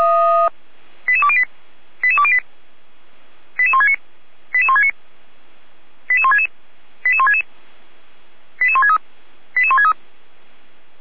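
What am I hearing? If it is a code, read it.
Five-tone selective call (ZVEI), heard as quick runs of five tones, tones of about 70 ms. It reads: 88208 88208 88169 88169 88166 88166 88163 88163
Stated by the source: ZVEI five-tone selective call 88203, received with an SDR.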